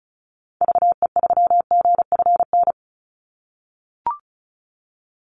Morse code, heard as a keyed tone, about 34 wpm, 700 Hz. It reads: VE3GFN
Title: Call Signs - 35wpm